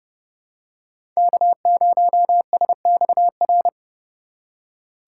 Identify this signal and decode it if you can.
Morse code, heard as a keyed tone, about 30 wpm, 700 Hz. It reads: K0SXR
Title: USA Call Signs - 5 Characters - 2x Character Spacing - 30wpm